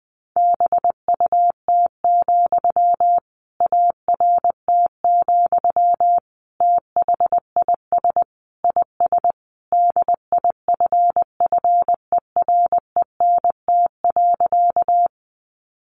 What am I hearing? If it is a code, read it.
Morse code, heard as a keyed tone, about 20 wpm, 700 Hz. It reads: BUT, ART, THIS IS DIFFERENT.
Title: Sentences from Top 600 Words 20wpm